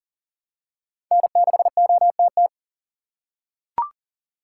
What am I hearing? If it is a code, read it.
Morse code, heard as a keyed tone, about 40 wpm, 700 Hz. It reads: N6OTT
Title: USA Call Signs - 5 Characters - 4x Character Spacing - 40wpm